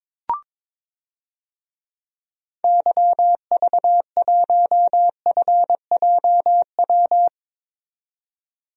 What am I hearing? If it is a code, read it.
Morse code, heard as a keyed tone, about 22 wpm, 700 Hz. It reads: YV1FJW
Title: DXCC-100 Call Signs - 6 Characters - 22wpm